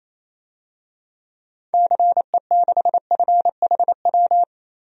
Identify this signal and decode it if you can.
Morse code, heard as a keyed tone, about 28 wpm, 700 Hz. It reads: CE6FHW
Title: DXCC-100 Call Signs - 6 Characters - 28wpm